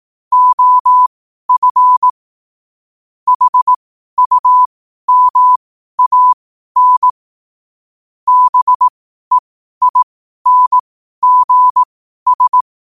Morse code, heard as a keyed tone, about 18 wpm, spaced as for 12 wpm, 1000 Hz. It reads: OF HUMAN BEINGS
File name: qotdfiles / 12wpm